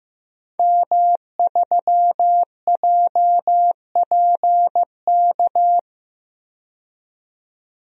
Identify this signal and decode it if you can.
Morse code, heard as a keyed tone, about 15 wpm, 700 Hz. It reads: M3JPK
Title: DXCC-100 Call Signs - 5 Characters - 15wpm